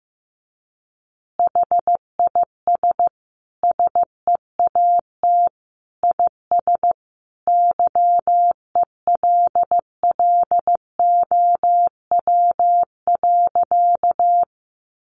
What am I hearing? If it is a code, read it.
Morse code, heard as a keyed tone, about 15 wpm, 700 Hz. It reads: HIS SEAT IS YELLOW.
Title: Additional Easy Phrases 15wpm